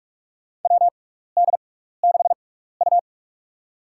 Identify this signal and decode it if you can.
Morse code, heard as a keyed tone, about 46 wpm, spaced as for 15 wpm, 700 Hz. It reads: WD6U
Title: USA Call Signs - 4 Characters - 6x Character Spacing - 45wpm